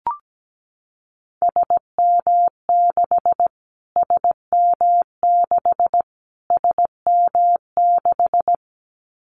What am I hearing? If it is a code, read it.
Morse code, heard as a keyed tone, about 17 wpm, 700 Hz. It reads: SM6 SM6 SM6
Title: Callsign Prefixes for Top 25 DXCC - Repeated 3x - 17wpm